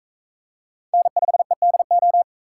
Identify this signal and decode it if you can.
Morse code, heard as a keyed tone, about 41 wpm, 700 Hz. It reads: N5EDO